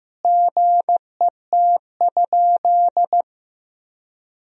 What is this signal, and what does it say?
Morse code, heard as a keyed tone, about 15 wpm, 700 Hz. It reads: GET?
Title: Sentences from Top 1000 Words - Review Part 3 - 15wpm